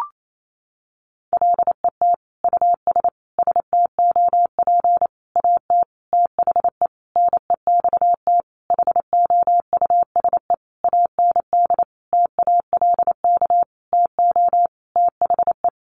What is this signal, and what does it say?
Morse code, heard as a keyed tone, about 28 wpm, 700 Hz. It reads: LET US STOP AT THE NEXT HOUSE AND TALK TO THE
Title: Sentences from Top 400 Words 28wpm